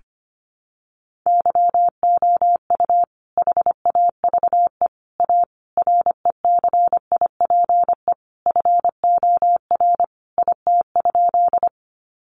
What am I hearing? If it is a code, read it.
Morse code, heard as a keyed tone, about 25 wpm, 700 Hz. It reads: YOU HAVE A RECIPE FOR IT?
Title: Sentences from Top 1200 Words 25wpm